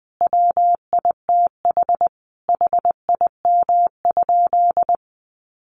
Morse code, heard as a keyed tone, about 20 wpm, 700 Hz. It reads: WITH HIM?
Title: Sentences from Top 1400 Words 20wpm